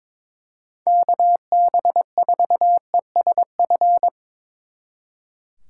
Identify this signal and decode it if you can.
Morse code, heard as a keyed tone, about 24 wpm, 700 Hz. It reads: KB4ESF
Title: Speed Racing - USA 2x3 Call Signs 22wpm